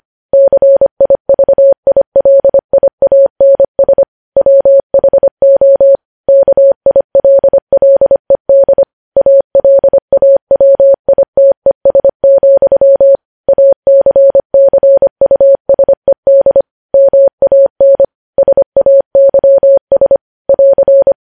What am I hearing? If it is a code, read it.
Morse code, heard as a keyed tone, about 25 wpm, 550 Hz. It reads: CIVILIANS WHO KILLED ALAWITES, ACCUSED MAN SAYS <AR>
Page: News Headlines in Morse Code at 25 WPM